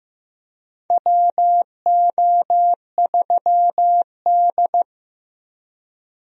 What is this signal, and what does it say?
Morse code, heard as a keyed tone, about 15 wpm, 700 Hz. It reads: WO3D
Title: USA Call Signs - 4 Characters - 15wpm